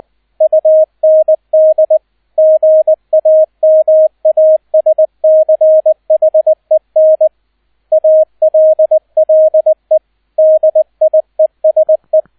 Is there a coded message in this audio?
Yes: Morse code from 0.4 to 12.2 s, heard as a keyed tone, about 20 wpm, 630 Hz, with radio noise. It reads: UND GAMASCHEN ALLE DIESE